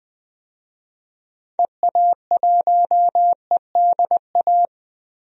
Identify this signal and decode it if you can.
Morse code, heard as a keyed tone, about 20 wpm, 700 Hz. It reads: EA1EDA
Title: DXCC-100 Call Signs - 6 Characters - 20wpm